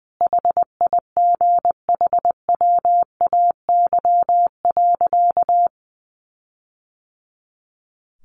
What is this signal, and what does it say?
Morse code, heard as a keyed tone, about 19 wpm, 700 Hz. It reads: HIGHWAY.
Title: Sentences from Top 1400 Words 20wpm